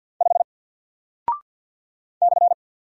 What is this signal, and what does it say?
Morse code, heard as a keyed tone, about 48 wpm, 700 Hz. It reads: 5 /
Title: Single Character (Letter:Number:Punctuation:Prosign) Alt 50wpm